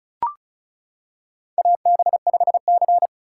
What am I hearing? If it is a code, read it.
Morse code, heard as a keyed tone, about 34 wpm, 700 Hz. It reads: AB5C
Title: K1USN SST QSO Practice - 35wpm